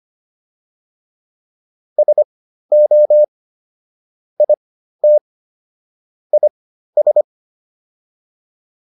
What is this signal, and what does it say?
Morse code, heard as a keyed tone, about 25 wpm, spaced as for 13 wpm, 600 Hz. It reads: SO IT IS